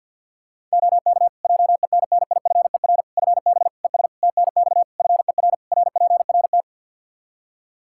Morse code, heard as a keyed tone, about 52 wpm, 700 Hz. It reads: OK JENNIFER FB ES TNX FER RPRT
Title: Practice QSO - Exchange 2 - Info - 50wpm